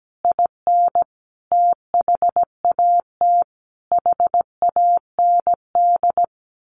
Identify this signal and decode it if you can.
Morse code, heard as a keyed tone, about 17 wpm, 700 Hz. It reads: IN THAT HAND
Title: A Sound Of Thunder 17wpm